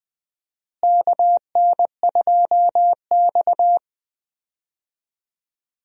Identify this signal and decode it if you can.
Morse code, heard as a keyed tone, about 20 wpm, 700 Hz. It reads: KN2X
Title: USA Call Signs - 4 Characters - Repeated 3x - 20wpm